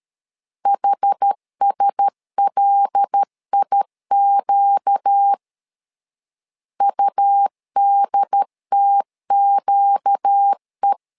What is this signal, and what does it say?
Morse code, heard as a keyed tone, about 13 wpm, 800 Hz. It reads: HSLIQ UDTQE
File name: LEZIONI / 5 LBVC / 5 L 12wpm 1913